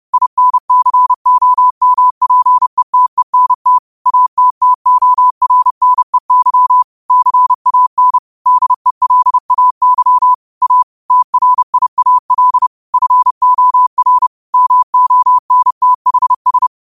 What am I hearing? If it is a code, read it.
Morse code, heard as a keyed tone, about 30 wpm, 1000 Hz. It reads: INCOMPETENT ATTORNEY CAN DELAY A TRIAL FOR MONTHS
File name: qotdfiles / 30wpm